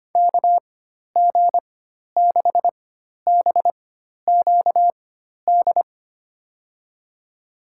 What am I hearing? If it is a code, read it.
Morse code, heard as a keyed tone, about 25 wpm, spaced as for 12 wpm, 700 Hz. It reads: KG6BQD
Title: USA Call Signs - 6 Characters - 4x Character Spacing - 25wpm